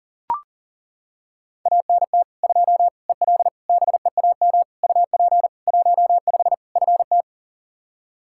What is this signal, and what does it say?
Morse code, heard as a keyed tone, about 40 wpm, 700 Hz. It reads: ANT 2 EL BEAM UP 15 FT
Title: Practice QSO - Exchange 2 - Info - 40wpm